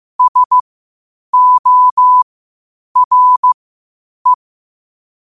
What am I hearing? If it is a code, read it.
Morse code, heard as a keyed tone, about 15 wpm, spaced as for 8 wpm, 1000 Hz. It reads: SORE
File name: qotdfiles / 8wpm